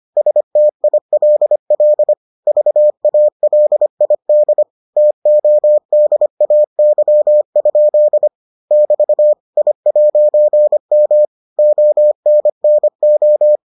Morse code, heard as a keyed tone, about 25 wpm, 600 Hz. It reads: STILL VALID TODAY? <BT> I'M ONNO